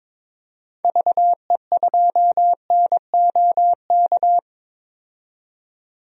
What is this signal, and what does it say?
Morse code, heard as a keyed tone, about 22 wpm, 700 Hz. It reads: VE2NOK
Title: DXCC-100 Call Signs - 6 Characters - 22wpm